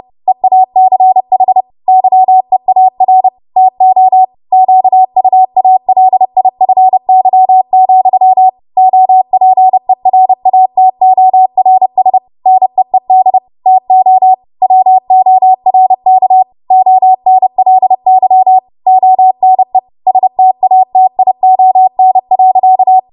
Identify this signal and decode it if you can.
Morse code, heard as a keyed tone, about 30 wpm, 750 Hz. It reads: EACH YEAR TO QUALIFY, OPERATORS NEED TO WORK ONLY ONE STATION.
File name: Morse / 30WPM